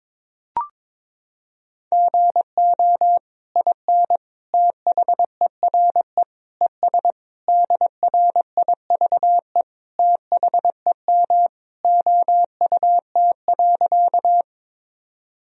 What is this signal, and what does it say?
Morse code, heard as a keyed tone, about 21 wpm, 700 Hz. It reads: GO IN THERE ES DRIVE THEM OUT.
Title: Sentences from Top 500 Words and Abbreviations 22wpm